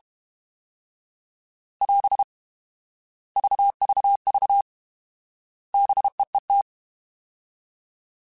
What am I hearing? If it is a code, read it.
Morse code, heard as a keyed tone, about 32 wpm, 800 Hz. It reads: <AS> VVV BEET
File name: Alphabetical / Alphabetical mp3s / Bs @30